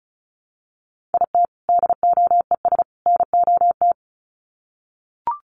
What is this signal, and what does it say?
Morse code, heard as a keyed tone, about 34 wpm, 700 Hz. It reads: IT DOES NOT
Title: Sets of 3 Words 35wpm